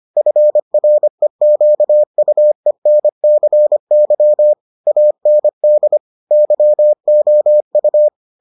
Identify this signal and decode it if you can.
Morse code, heard as a keyed tone, about 25 wpm, 600 Hz. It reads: FREQUENCY AND YOU